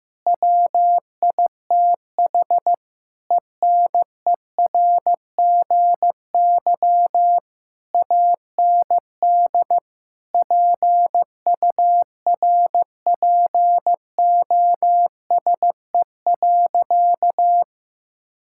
Morse code, heard as a keyed tone, about 15 wpm, 700 Hz. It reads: WITH ENERGY AND PURPOSE.
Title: Sentences from Top 1000 Words - Review Part 6 - 15wpm